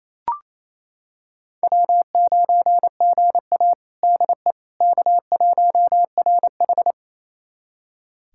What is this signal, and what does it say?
Morse code, heard as a keyed tone, about 28 wpm, 700 Hz. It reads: W9GA DE K1RH
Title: USA Callsign DE Callsign - 4 Characters - 28wpm